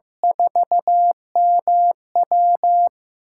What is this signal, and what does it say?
Morse code, heard as a keyed tone, about 16 wpm, 700 Hz. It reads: 4MW
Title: USA Call Signs - 8x Character Spacing - 15wpm